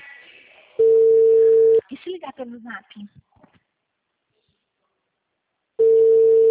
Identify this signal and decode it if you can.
Ringback tone (China, 450 Hz), with speech.